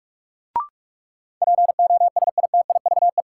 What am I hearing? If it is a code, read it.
Morse code, heard as a keyed tone, about 47 wpm, 700 Hz. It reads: POSITIVE
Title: Sentences from Top 1000 Words - Review Part 5 - New Word Drill - 45wpm